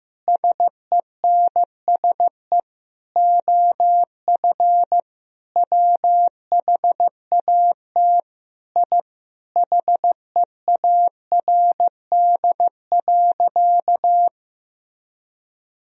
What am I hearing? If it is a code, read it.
Morse code, heard as a keyed tone, about 16 wpm, 700 Hz. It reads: SENSE OF WHAT I HEARD.
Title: Sentences from Top 1000 Words - Review Part 4 - 15wpm